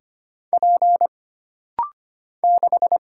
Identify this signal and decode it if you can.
Morse code, heard as a keyed tone, about 25 wpm, 700 Hz. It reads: P 6
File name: Single Character (Letter:Number:Punctuation:Prosign) - Rapid-Fire - 25wpm